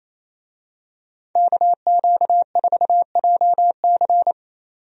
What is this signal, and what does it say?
Morse code, heard as a keyed tone, about 28 wpm, 700 Hz. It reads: KQ4JC
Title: USA Call Signs - 5 Characters - 28wpm